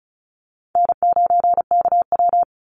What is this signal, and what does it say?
Morse code, heard as a keyed tone, about 34 wpm, 700 Hz. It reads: N9KW
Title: NA SOTA Chaser Callsigns 35wpm